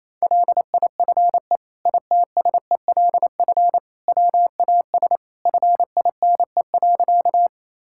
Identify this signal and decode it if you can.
Morse code, heard as a keyed tone, about 28 wpm, 700 Hz. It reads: LIFE ITSELF WAS FINE.
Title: Sentences from Top 1000 Words - Review Part 5 - 28wpm